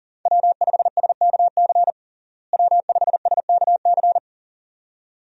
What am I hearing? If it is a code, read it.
Morse code, heard as a keyed tone, about 40 wpm, 700 Hz. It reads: W5SKC W5SKC
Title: USA Call Signs - 5 Characters - Repeated 2x - 40wpm